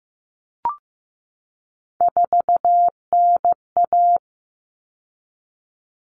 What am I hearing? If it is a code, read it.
Morse code, heard as a keyed tone, about 15 wpm, 700 Hz. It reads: 4NA
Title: Common Cut Numbers - 3-Digit 15wpm